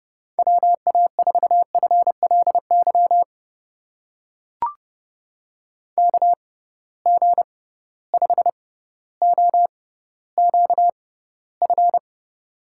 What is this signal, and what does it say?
Morse code, heard as a keyed tone, about 30 wpm, spaced as for 15 wpm, 700 Hz. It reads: WA4FLY KG5OQF
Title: USA Call Signs - 6 Characters - 6x Character Spacing - 30wpm